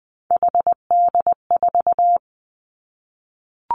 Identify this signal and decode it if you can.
Morse code, heard as a keyed tone, about 20 wpm, 700 Hz. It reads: HD4
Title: Three Challenging Letter-Number Combinations 20wpm